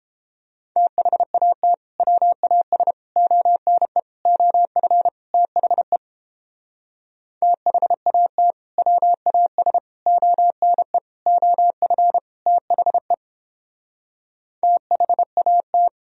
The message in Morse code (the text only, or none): THAT WAS ONE OF THE THAT WAS ONE OF THE THAT